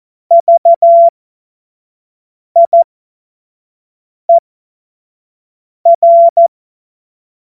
Morse code, heard as a keyed tone, about 14 wpm, spaced as for 5 wpm, 680 Hz. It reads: VIER
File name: Morsekurs / Witze / Witz 31. Mai 2022 25 F70